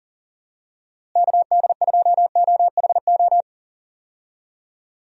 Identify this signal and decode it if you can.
Morse code, heard as a keyed tone, about 40 wpm, 700 Hz. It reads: KD2OHO
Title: USA Call Signs - 6 Characters - Repeated 2x - 40wpm